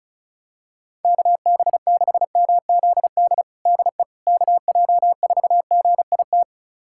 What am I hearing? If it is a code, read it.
Morse code, heard as a keyed tone, about 34 wpm, 700 Hz. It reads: KB6MZD DE KJ4GIT
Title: USA Callsign DE Callsign - 6 Characters - 35wpm